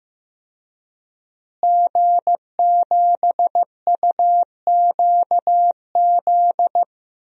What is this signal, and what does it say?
Morse code, heard as a keyed tone, about 15 wpm, 700 Hz. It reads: G7UQZ
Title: DXCC-100 Call Signs - 5 Characters - Repeated 2x - 15wpm